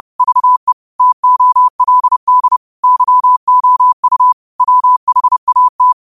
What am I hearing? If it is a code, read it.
Morse code, heard as a keyed tone, about 30 wpm, 1000 Hz. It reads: VE TOLD YOU WHAT
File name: qotdfiles / 30wpm